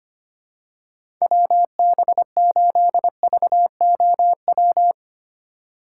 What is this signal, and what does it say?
Morse code, heard as a keyed tone, about 25 wpm, 700 Hz. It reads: WB8VOW
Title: USA Call Signs - 6x Character Spacing - 25wpm